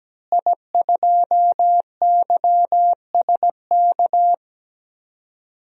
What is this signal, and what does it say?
Morse code, heard as a keyed tone, about 17 wpm, 700 Hz. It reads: I2YSK
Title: DXCC-100 Call Signs - 5 Characters - Repeated 2x - 17wpm